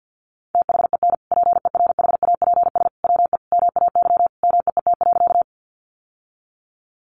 Morse code, heard as a keyed tone, about 50 wpm, 700 Hz. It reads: THEN PERHAPS WE MAY MEET.
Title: Sentences from Top 1100 Words 50wpm